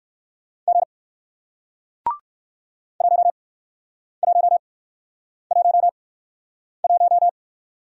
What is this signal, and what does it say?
Morse code, heard as a keyed tone, about 60 wpm, spaced as for 9 wpm, 700 Hz. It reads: G 1111